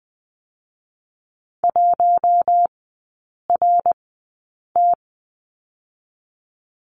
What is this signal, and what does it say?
Morse code, heard as a keyed tone, about 20 wpm, spaced as for 8 wpm, 700 Hz. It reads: 1RT
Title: 060 Lesson 11e - Letter h - Three Characters 20wpm